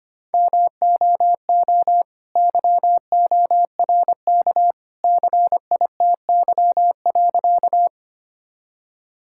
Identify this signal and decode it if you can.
Morse code, heard as a keyed tone, about 25 wpm, 700 Hz. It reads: MOO YORK CITY.
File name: practice / 25wpm / All-Age Appropriate Jokes - 25wpm